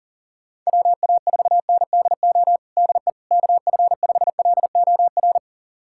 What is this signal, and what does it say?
Morse code, heard as a keyed tone, about 40 wpm, 700 Hz. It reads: WA4NDO DE KF5LOR
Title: USA Callsign DE Callsign 40wpm